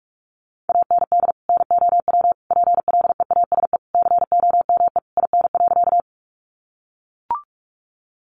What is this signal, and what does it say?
Morse code, heard as a keyed tone, about 44 wpm, 700 Hz. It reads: AND NOW PLEASE COME IN.
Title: Sentences from Top 1500 Words 45wpm